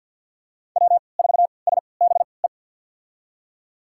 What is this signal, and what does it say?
Morse code, heard as a keyed tone, about 52 wpm, spaced as for 29 wpm, 700 Hz. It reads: W4SBE
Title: USA Call Signs - 5 Characters - 3x Character Spacing - 50wpm